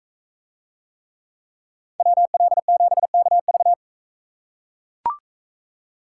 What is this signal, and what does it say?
Morse code, heard as a keyed tone, about 40 wpm, 700 Hz. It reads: WL7KV